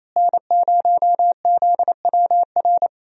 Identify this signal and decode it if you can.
Morse code, heard as a keyed tone, about 28 wpm, 700 Hz. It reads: N0ZWR
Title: USA Call Signs - 5 Characters - Repeated 3x - 28wpm